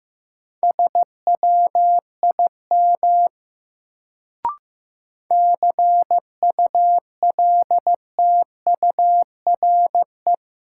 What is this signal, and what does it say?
Morse code, heard as a keyed tone, about 15 wpm, 700 Hz. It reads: SWIM CULTURE
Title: Sentences from Top 1000 Words - Review Part 6 - New Word Drill - 15wpm